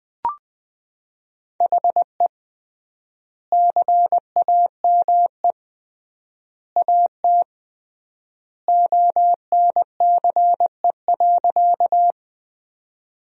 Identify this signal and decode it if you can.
Morse code, heard as a keyed tone, about 21 wpm, 700 Hz. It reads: HE CAME AT ONCE.
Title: 3x Word-Spacing - Sentences from Top 300 Words - 20wpm